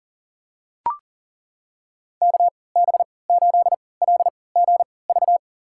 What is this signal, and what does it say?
Morse code, heard as a keyed tone, about 40 wpm, spaced as for 23 wpm, 700 Hz. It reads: KB8LGV
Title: USA Call Signs - 6 Characters - 3x Character Spacing - 40wpm